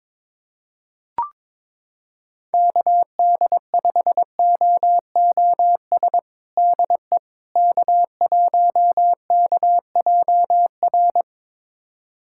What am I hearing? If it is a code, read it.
Morse code, heard as a keyed tone, about 21 wpm, 700 Hz. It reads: KD5OOS DE K1KJR
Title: USA Callsign DE Callsign 22wpm